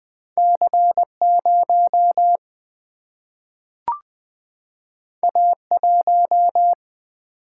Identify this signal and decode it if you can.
Morse code, heard as a keyed tone, about 20 wpm, 700 Hz. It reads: C 0 A 1
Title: Single-Letter Single-Number 20wpm